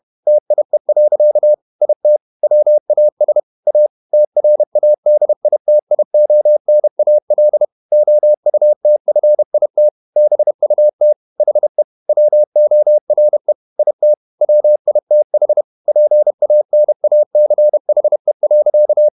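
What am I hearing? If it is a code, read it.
Morse code, heard as a keyed tone, about 31 wpm, 600 Hz. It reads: TIE. IT WAS A TRADITIONAL OUTFIT BUT HE WORE IT WITH PANACHE.
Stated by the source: Morse code at 30 wpm, the novel Never by Ken Follett in English, from CW med Gnister.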